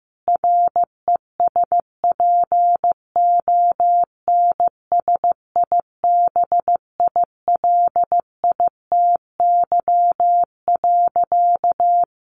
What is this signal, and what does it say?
Morse code, heard as a keyed tone, about 15 wpm, 700 Hz. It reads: RESPONSIBILITY.
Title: Sentences from Top 1500 Words 15wpm